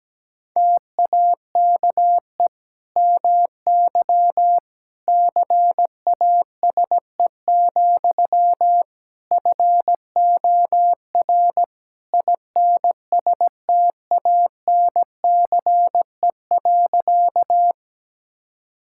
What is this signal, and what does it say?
Morse code, heard as a keyed tone, about 17 wpm, 700 Hz. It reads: TAKE MY CASE, FOR INSTANCE.
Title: Sentences from Top 1000 Words - Review Part 5 - 17wpm